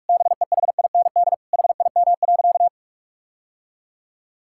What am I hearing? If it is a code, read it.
Morse code, heard as a keyed tone, about 41 wpm, 700 Hz. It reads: BEHIND HIM.
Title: Sentences from Top 1600 Words 45wpm